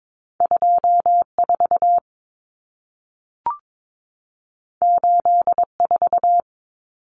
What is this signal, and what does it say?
Morse code, heard as a keyed tone, about 22 wpm, 700 Hz. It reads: 24 84
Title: Numbers - 2-Digit 22wpm